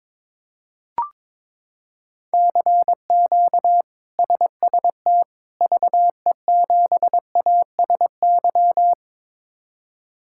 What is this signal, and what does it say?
Morse code, heard as a keyed tone, about 22 wpm, 700 Hz. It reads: CQ SST VE7ASY
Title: K1USN SST QSO Practice - 22wpm